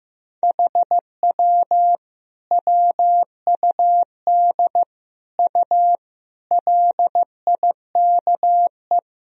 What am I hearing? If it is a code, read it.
Morse code, heard as a keyed tone, about 16 wpm, 700 Hz. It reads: HW WUD U LIKE
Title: Sentences from Top 100 Words and Abbreviations 15wpm